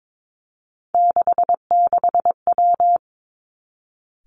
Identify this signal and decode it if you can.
Morse code, heard as a keyed tone, about 22 wpm, 700 Hz. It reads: 66W
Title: Three Challenging Letter-Number Combinations 22wpm